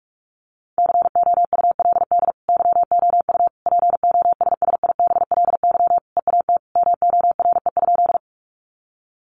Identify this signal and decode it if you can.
Morse code, heard as a keyed tone, about 46 wpm, 700 Hz. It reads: COULD YOU POSSIBLY EAT MORE?
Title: Sentences from Top 1300 Words 45wpm